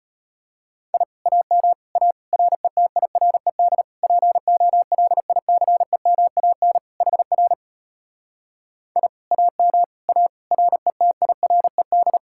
Morse code, heard as a keyed tone, about 33 wpm, 700 Hz. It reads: I AM A RETIRED POLICEMAN HR I AM A RETIRED